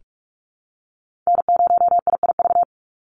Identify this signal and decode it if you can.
Morse code, heard as a keyed tone, about 46 wpm, 690 Hz. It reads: N0IIV
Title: USA Call Signs 45wpm